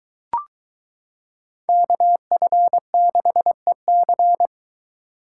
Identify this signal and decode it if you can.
Morse code, heard as a keyed tone, about 22 wpm, 700 Hz. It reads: KF6EC